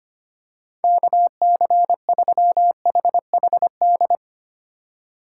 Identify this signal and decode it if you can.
Morse code, heard as a keyed tone, about 25 wpm, 700 Hz. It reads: KC3HHD
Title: USA Call Signs - 6 Characters - 3x Character Spacing - 25wpm